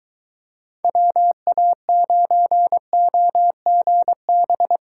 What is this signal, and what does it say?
Morse code, heard as a keyed tone, about 23 wpm, 700 Hz. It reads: WA9OGB